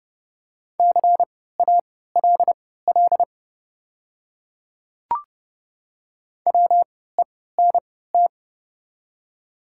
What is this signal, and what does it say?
Morse code, heard as a keyed tone, about 30 wpm, spaced as for 17 wpm, 700 Hz. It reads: CALL WENT
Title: 3x Character Spacing - Top 200 Words - 30wpm